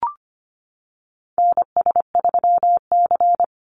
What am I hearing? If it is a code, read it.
Morse code, heard as a keyed tone, about 25 wpm, 700 Hz. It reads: NS3C
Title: Call Signs Repeated 2x - 25wpm